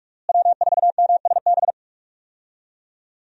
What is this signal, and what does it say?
Morse code, heard as a keyed tone, about 46 wpm, 700 Hz. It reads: W4MSB
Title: USA Call Signs - 5 Characters - Repeated 2x - 45wpm